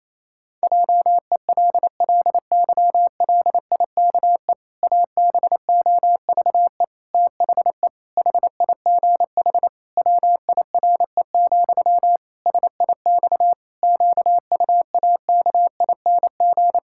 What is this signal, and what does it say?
Morse code, heard as a keyed tone, about 28 wpm, 700 Hz. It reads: JELLYLIKE ABOVE THE HIGH WIRE, SIX QUAKING
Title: Pangram Challenge v2 - 28wpm